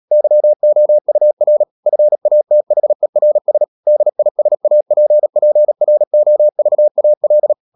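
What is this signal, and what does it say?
Morse code, heard as a keyed tone, about 37 wpm, 600 Hz. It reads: YOUR FATHERS DISAPPROVAL
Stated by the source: Morse practice at 36 wpm, text of Ken Follett's novel Never, from CW med Gnister.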